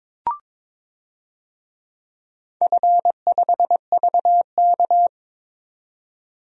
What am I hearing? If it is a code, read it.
Morse code, heard as a keyed tone, about 22 wpm, 700 Hz. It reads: F5VK
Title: DXCC-100 Call Signs - 4 Characters - 22wpm